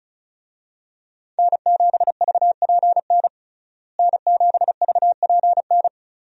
Morse code, heard as a keyed tone, about 36 wpm, 700 Hz. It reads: N7VPN N7VPN
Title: USA Call Signs - 5 Characters - Repeated 2x - 35wpm